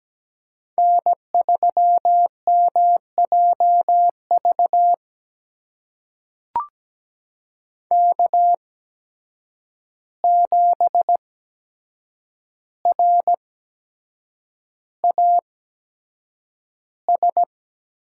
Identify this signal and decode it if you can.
Morse code, heard as a keyed tone, about 17 wpm, spaced as for 7 wpm, 700 Hz. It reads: N3MJV K7RAS
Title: USA Call Signs - 5 Characters - 8x Character Spacing - 17wpm